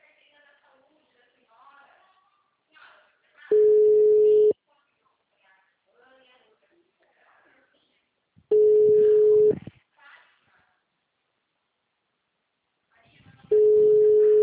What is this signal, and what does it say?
Ringback tone (Europe, 425 Hz), with speech.